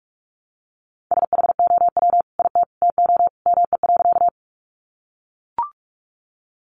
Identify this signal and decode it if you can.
Morse code, heard as a keyed tone, about 44 wpm, 700 Hz. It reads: SHOW IT TO ME.
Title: Sentences from Top 200 Words 45wpm